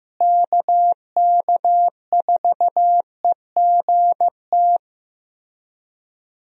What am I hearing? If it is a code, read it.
Morse code, heard as a keyed tone, about 15 wpm, 700 Hz. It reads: KK4EGT
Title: USA Call Signs - 6 Characters - Repeated 3x - 15wpm